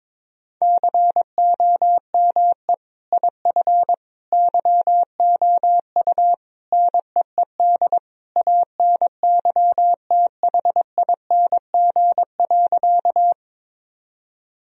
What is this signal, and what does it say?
Morse code, heard as a keyed tone, about 22 wpm, 700 Hz. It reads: COME IF YOU NEED ANYTHING.
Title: Sentences from Top 1000 Words - Review Part 1 - 22wpm